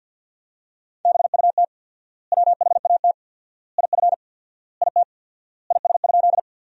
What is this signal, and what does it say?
Morse code, heard as a keyed tone, about 50 wpm, 700 Hz. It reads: BUT WHAT IF IT IS?
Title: 4x Word-Spacing - Sentences from Top 100 Words - 50wpm